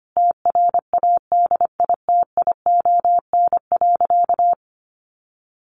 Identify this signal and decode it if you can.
Morse code, heard as a keyed tone, about 25 wpm, 700 Hz. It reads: TRADITION.
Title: Sentences from Top 1500 Words 25wpm